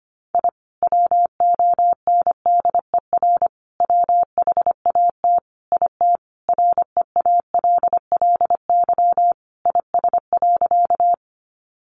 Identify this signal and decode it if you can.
Morse code, heard as a keyed tone, about 25 wpm, 700 Hz. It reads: I WONDER WHAT IT REALLY IS.
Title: Sentences from Top 600 Words 25wpm